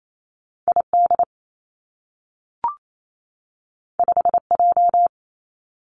Morse code, heard as a keyed tone, about 28 wpm, 700 Hz. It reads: I D 5 J
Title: Two Challenging Letter-Number Combinations 28wpm